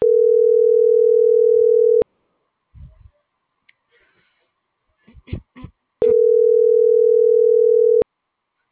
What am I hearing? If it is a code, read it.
Ringback tone (North America), 440 + 480 Hz.